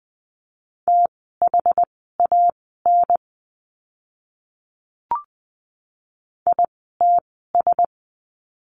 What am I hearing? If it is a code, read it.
Morse code, heard as a keyed tone, about 20 wpm, spaced as for 15 wpm, 700 Hz. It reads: THAN ITS
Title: 206 Lesson 33f - Letter j - Words 20wpm